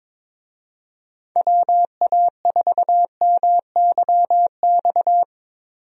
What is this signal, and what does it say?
Morse code, heard as a keyed tone, about 21 wpm, 700 Hz. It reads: WA4MYX
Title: USA Call Signs - 6 Characters - 22wpm